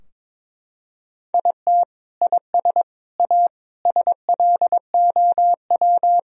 Morse code, heard as a keyed tone, about 21 wpm, 700 Hz. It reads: IT IS A SLOW
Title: Sentences from Top 1000 Words - Review Part 2 - 22wpm